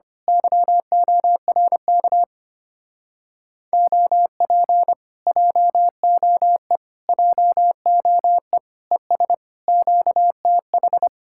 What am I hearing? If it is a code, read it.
Morse code, heard as a keyed tone, about 25 wpm, 700 Hz. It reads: YORK OP JOE JOE ES QTH
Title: Speed-Racing Common QSO Phrases - 25wpm